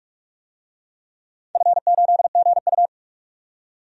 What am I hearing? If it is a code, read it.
Morse code, heard as a keyed tone, about 44 wpm, 700 Hz. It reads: F8GU
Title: DXCC-100 Call Signs - 4 Characters - 45wpm